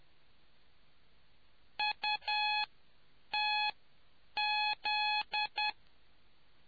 Morse code, heard as a keyed tone, about 10 wpm, spaced as for 7 wpm, 840 Hz. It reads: UTZ